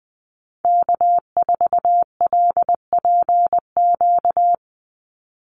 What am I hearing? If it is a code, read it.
Morse code, heard as a keyed tone, about 20 wpm, 700 Hz. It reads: K4LPQ
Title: NA SOTA Activator Callsigns 20wpm